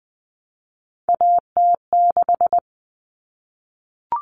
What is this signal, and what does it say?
Morse code, heard as a keyed tone, about 20 wpm, 700 Hz. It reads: AT6
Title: Common Cut Numbers - 3-Digit 20wpm